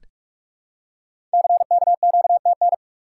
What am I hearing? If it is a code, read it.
Morse code, heard as a keyed tone, about 44 wpm, 700 Hz. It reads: CKQTN
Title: Code Groups - 5 Letters - 45wpm